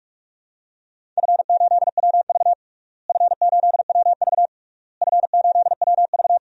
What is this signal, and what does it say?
Morse code, heard as a keyed tone, about 46 wpm, 700 Hz. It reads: F8WV F8WV F8WV
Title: DXCC-100 Call Signs - 4 Characters - Repeated 3x - 45wpm